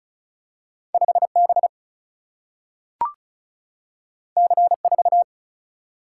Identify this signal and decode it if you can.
Morse code, heard as a keyed tone, about 36 wpm, 700 Hz. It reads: 5B C4
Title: Single-Letter Single-Number 35wpm